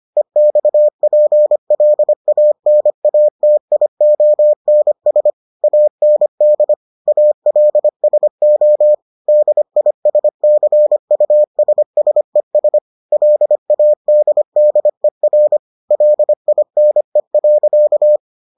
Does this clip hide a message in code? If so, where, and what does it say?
Yes: Morse code from 0.2 to 18.2 s, heard as a keyed tone, about 25 wpm, 600 Hz. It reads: EXPLANATIONS AND ALSO DISCUSSES LADDER LINE.